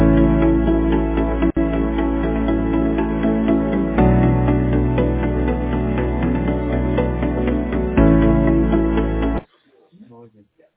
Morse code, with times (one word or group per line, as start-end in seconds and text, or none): none